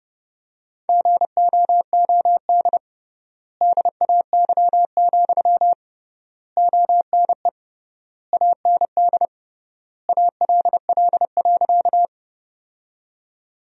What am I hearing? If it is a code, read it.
Morse code, heard as a keyed tone, about 30 wpm, 700 Hz. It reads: GOOD DAY, ONE AND ALL.
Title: 3x Word-Spacing - Sentences from Top 200 Words - 30wpm